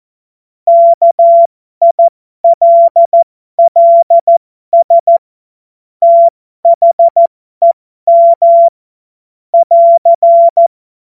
Morse code, heard as a keyed tone, about 14 wpm, spaced as for 12 wpm, 680 Hz. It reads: KILLS THEM <AR>
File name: Morsekurs / Witze